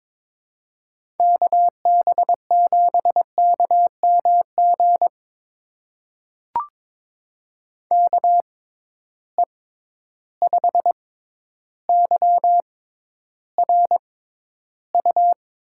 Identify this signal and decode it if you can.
Morse code, heard as a keyed tone, about 22 wpm, spaced as for 11 wpm, 700 Hz. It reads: KB7KMG KE5YRU
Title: USA Call Signs - 6 Characters - 6x Character Spacing - 22wpm